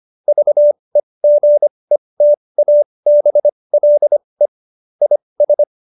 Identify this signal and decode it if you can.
Morse code, heard as a keyed tone, about 25 wpm, spaced as for 20 wpm, 590 Hz. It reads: VEGETABLE IS
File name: Jokes-25-char-20-eff